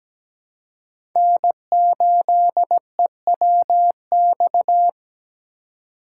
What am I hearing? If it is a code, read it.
Morse code, heard as a keyed tone, about 17 wpm, 700 Hz. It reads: N8EWX
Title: K1USN SST QSO Practice - 17wpm